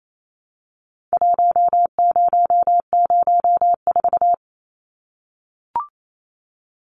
Morse code, heard as a keyed tone, about 28 wpm, 700 Hz. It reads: 1004
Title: Numbers - 4-Digit 28wpm